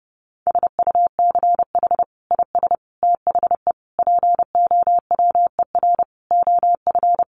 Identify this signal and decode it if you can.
Morse code, heard as a keyed tone, about 30 wpm, 700 Hz. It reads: SUCH IS THE POWER OF
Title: Sentences from Top 1400 Words 30wpm